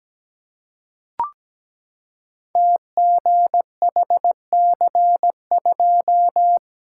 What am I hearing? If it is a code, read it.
Morse code, heard as a keyed tone, about 17 wpm, 700 Hz. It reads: TGHC2
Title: Code Groups - 5 Letter:Numbers - 17wpm